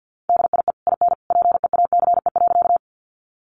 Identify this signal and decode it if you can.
Morse code, heard as a keyed tone, about 48 wpm, 700 Hz. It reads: DIE IN PEACE.
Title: Sentences from Top 1500 Words 50wpm